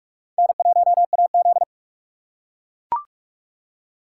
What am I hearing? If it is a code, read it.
Morse code, heard as a keyed tone, about 42 wpm, 700 Hz. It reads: N1AZ